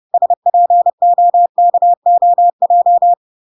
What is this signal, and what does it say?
Morse code, heard as a keyed tone, about 30 wpm, 700 Hz. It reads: SPOKOJ